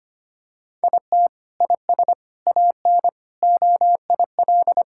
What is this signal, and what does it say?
Morse code, heard as a keyed tone, about 25 wpm, 700 Hz. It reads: IT IS AN OIL
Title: Sentences from Top 1000 Words - Review Part 6 - 25wpm